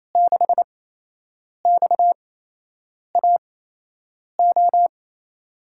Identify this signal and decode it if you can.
Morse code, heard as a keyed tone, about 28 wpm, spaced as for 8 wpm, 700 Hz. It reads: 6XAO
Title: USA Call Signs - 5 Characters - 8x Character Spacing - 28wpm